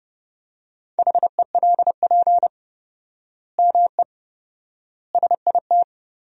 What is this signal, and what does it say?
Morse code, heard as a keyed tone, about 30 wpm, 700 Hz. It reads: HELP ME SIT
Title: 4x Word-Spacing - Sentences from Top 500 Words - 30wpm